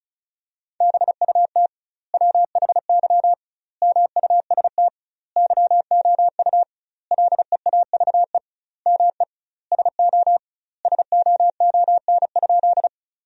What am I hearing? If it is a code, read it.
Morse code, heard as a keyed tone, about 34 wpm, 700 Hz. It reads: BUT WHY MUST YOU LEAVE ME SO SOON?
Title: 2x Word-Spacing - Sentences from Top 400 Words - 35wpm